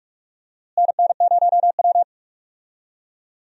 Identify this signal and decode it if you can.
Morse code, heard as a keyed tone, about 44 wpm, 700 Hz. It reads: NN0W
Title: USA Call Signs - 4 Characters - Repeated 3x - 45wpm